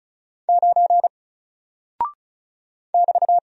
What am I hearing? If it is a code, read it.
Morse code, heard as a keyed tone, about 36 wpm, 700 Hz. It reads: 9 <BT>
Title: Single Character (Letter:Number:Punctuation:Prosign) 35wpm